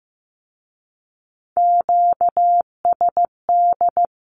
Morse code, heard as a keyed tone, about 15 wpm, 700 Hz. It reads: QSD
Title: Q-Codes with Spoken Meaning - 15wpm